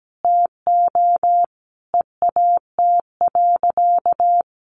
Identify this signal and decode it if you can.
Morse code, heard as a keyed tone, about 17 wpm, 700 Hz. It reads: TO EAT.
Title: Sentences from Top 1300 Words 17wpm